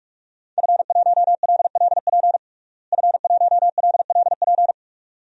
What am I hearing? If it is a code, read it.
Morse code, heard as a keyed tone, about 43 wpm, 700 Hz. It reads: F1LLP F1LLP
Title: DXCC-100 Call Signs - 5 Characters - Repeated 2x - 45wpm